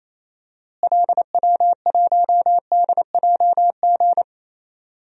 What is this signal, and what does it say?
Morse code, heard as a keyed tone, about 28 wpm, 700 Hz. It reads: LW1DJG
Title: DXCC-100 Call Signs - 6 Characters - Repeated 2x - 28wpm